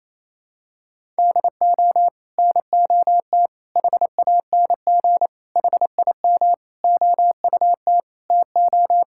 Morse code, heard as a keyed tone, about 28 wpm, 700 Hz. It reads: DO NOT HANG HIM OUT TO
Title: Sentences from Top 1600 Words 28wpm